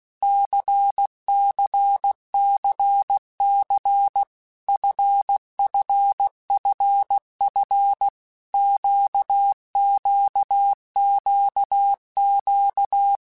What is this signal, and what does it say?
Morse code, heard as a keyed tone, about 16 wpm, 800 Hz. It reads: CCCC FFFF QQQQ
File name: Basic Letter Groups By7Groups / LFCYQ / LFCYQ@15